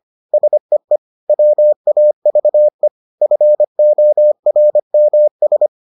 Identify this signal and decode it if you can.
Morse code, heard as a keyed tone, about 25 wpm, 600 Hz. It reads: SEE WAVE FORMS